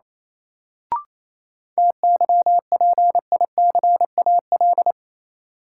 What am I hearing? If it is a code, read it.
Morse code, heard as a keyed tone, about 28 wpm, 700 Hz. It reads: TYPICAL
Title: Sentences from Top 1100 Words - New Word Drill Alt - 28wpm